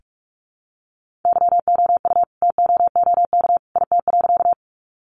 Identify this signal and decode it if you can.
Morse code, heard as a keyed tone, about 44 wpm, 700 Hz. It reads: YOU TOOK IT.
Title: Sentences from Top 300 Words 45wpm